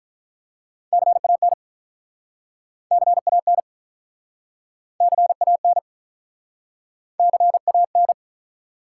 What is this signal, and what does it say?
Morse code, heard as a keyed tone, about 48 wpm, 700 Hz. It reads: CAN CAN CAN CAN